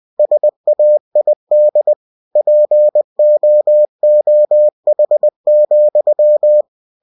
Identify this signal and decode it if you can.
Morse code, heard as a keyed tone, about 20 wpm, 600 Hz. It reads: SAID POOH,
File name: cw-books / winnie-the-pooh / winnie-20wpm-page-0003